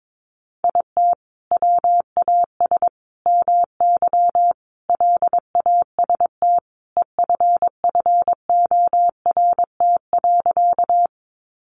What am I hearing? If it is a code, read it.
Morse code, heard as a keyed tone, about 22 wpm, 700 Hz. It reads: IT WAS MY LAST EFFORT.
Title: Sentences from Top 1200 Words 22wpm